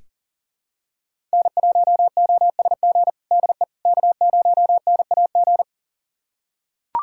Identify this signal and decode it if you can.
Morse code, heard as a keyed tone, about 40 wpm, 700 Hz. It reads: N1OSG DE K0NAG
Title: USA Callsign DE Callsign - 5 Characters - 40wpm